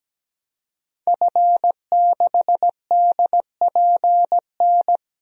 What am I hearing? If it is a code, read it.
Morse code, heard as a keyed tone, about 17 wpm, 700 Hz. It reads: F6DPN
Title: DXCC-100 Call Signs - 5 Characters - 17wpm